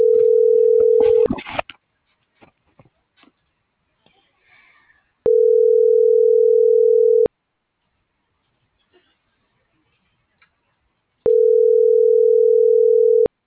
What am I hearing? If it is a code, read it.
Ringback tone (North America), 440 + 480 Hz.